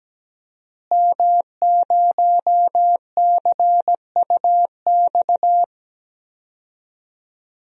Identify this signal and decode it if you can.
Morse code, heard as a keyed tone, about 17 wpm, 700 Hz. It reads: M0CUX
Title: DXCC-100 Call Signs - 5 Characters - Repeated 3x - 17wpm